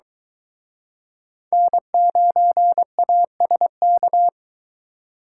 Morse code, heard as a keyed tone, about 23 wpm, 700 Hz. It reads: N9ASK